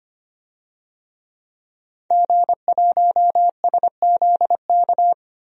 Morse code, heard as a keyed tone, about 25 wpm, 700 Hz. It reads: G1SZK
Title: DXCC-100 Call Signs - 5 Characters - 25wpm